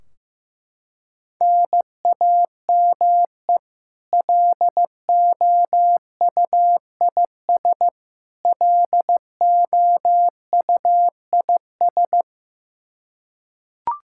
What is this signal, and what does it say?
Morse code, heard as a keyed tone, about 15 wpm, 700 Hz. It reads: NAME LOUIS LOUIS
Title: K1USN SST QSO Practice 15wpm